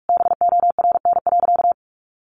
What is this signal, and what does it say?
Morse code, heard as a keyed tone, about 44 wpm, 700 Hz. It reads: BORN.
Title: Sentences from Top 1200 Words 45wpm